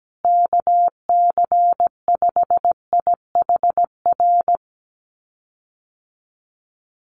Morse code, heard as a keyed tone, about 17 wpm, 700 Hz. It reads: KC5IHR
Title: USA Call Signs Repeated 3x 17wpm